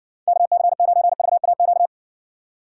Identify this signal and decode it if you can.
Morse code, heard as a keyed tone, about 60 wpm, 700 Hz. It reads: KC9VAQ